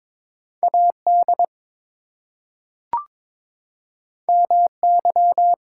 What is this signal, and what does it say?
Morse code, heard as a keyed tone, about 22 wpm, 700 Hz. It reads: AD MY
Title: Two Letter Words - 22wpm